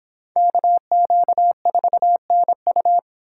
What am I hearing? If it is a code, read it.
Morse code, heard as a keyed tone, about 25 wpm, 700 Hz. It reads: KQ4NU